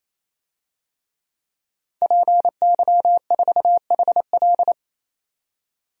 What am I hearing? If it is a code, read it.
Morse code, heard as a keyed tone, about 28 wpm, 700 Hz. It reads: PY4HL
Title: DXCC-100 Call Signs - 5 Characters - 28wpm